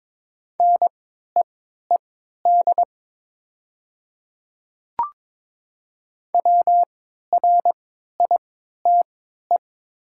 Morse code, heard as a keyed tone, about 22 wpm, spaced as for 12 wpm, 700 Hz. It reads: NEED WRITE
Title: 3x Character Spacing - Top 200 Words - 22wpm